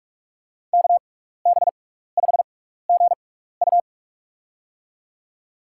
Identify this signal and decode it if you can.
Morse code, heard as a keyed tone, about 41 wpm, spaced as for 15 wpm, 700 Hz. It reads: KB5GU
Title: USA Call Signs - 6x Character Spacing - 45wpm